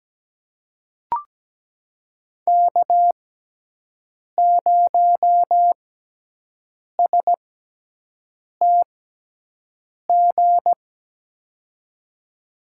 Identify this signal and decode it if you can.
Morse code, heard as a keyed tone, about 17 wpm, spaced as for 6 wpm, 700 Hz. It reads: K0STG
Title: USA Call Signs - 5 Characters - 6x Character Spacing - 17wpm